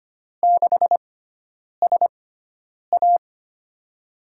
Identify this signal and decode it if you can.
Morse code, heard as a keyed tone, about 25 wpm, spaced as for 9 wpm, 700 Hz. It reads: 6SA
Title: USA Call Signs - 6x Character Spacing - 25wpm